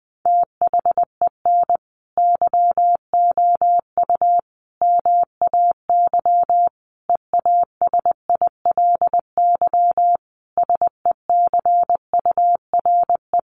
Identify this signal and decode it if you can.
Morse code, heard as a keyed tone, about 20 wpm, 700 Hz. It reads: THEN YOU MAY EASILY SECURE
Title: Sentences from Top 1200 Words 20wpm